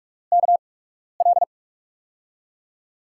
Morse code, heard as a keyed tone, about 43 wpm, spaced as for 12 wpm, 700 Hz. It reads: KL
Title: USA Call Signs - 6 Characters - 8x Character Spacing - 45wpm